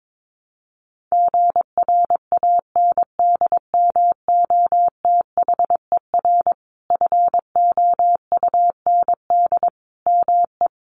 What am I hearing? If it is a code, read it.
Morse code, heard as a keyed tone, about 22 wpm, 700 Hz. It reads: GRANDMOTHER FOUND ME
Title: Sentences from Top 1200 Words 22wpm